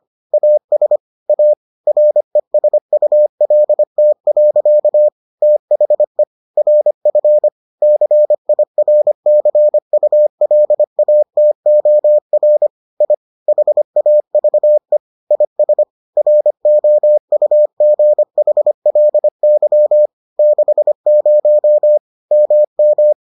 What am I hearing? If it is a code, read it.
Morse code, heard as a keyed tone, about 25 wpm, 600 Hz. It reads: AS A RESULT. THE RF CIRCULATOR I HAVE IS ROUGHLY 60 MM